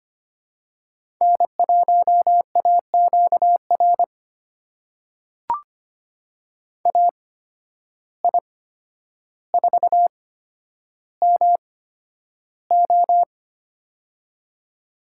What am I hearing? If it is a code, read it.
Morse code, heard as a keyed tone, about 25 wpm, spaced as for 11 wpm, 700 Hz. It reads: N1AQR AI4MO
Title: USA Call Signs - 8x Character Spacing - 25wpm